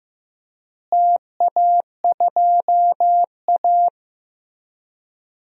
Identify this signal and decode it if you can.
Morse code, heard as a keyed tone, about 15 wpm, 700 Hz. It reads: TA2A
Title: DXCC-100 Call Signs - 4 Characters - Repeated 2x - 15wpm